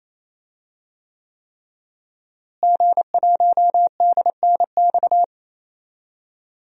Morse code, heard as a keyed tone, about 28 wpm, 700 Hz. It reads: G1DNX
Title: DXCC-100 Call Signs - 5 Characters - Repeated 3x - 28wpm